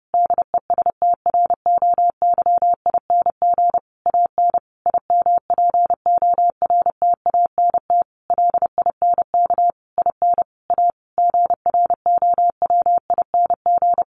Morse code, heard as a keyed tone, about 30 wpm, 700 Hz. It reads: DESTROYING AN IMPORTANT LINK IN A GROWING
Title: A Sound Of Thunder 30wpm